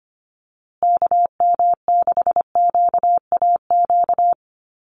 Morse code, heard as a keyed tone, about 25 wpm, 700 Hz. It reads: KM6QAQ
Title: USA Call Signs Repeated 3x 25wpm